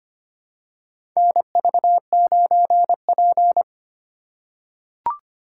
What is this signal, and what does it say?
Morse code, heard as a keyed tone, about 25 wpm, 700 Hz. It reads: NV9P
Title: USA Call Signs - 4 Characters - 8x Character Spacing - 25wpm